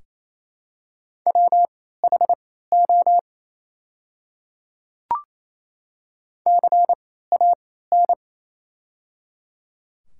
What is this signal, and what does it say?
Morse code, heard as a keyed tone, about 28 wpm, spaced as for 16 wpm, 700 Hz. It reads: WHO CAN
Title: 3x Character Spacing - Top 100 Words - 28wpm